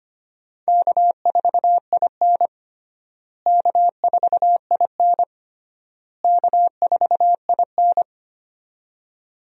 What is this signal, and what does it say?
Morse code, heard as a keyed tone, about 25 wpm, 700 Hz. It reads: K4IN K4IN K4IN
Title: USA Call Signs - 4 Characters - Repeated 3x - 25wpm